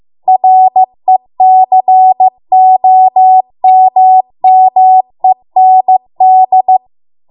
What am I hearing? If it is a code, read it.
Morse code, heard as a keyed tone, about 15 wpm, 750 Hz. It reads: RECOMMEND